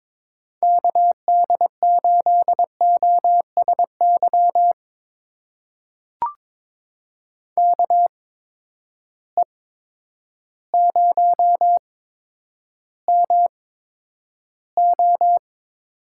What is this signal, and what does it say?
Morse code, heard as a keyed tone, about 22 wpm, spaced as for 10 wpm, 700 Hz. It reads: KD8OSY KE0MO
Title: USA Call Signs - 6 Characters - 8x Character Spacing - 22wpm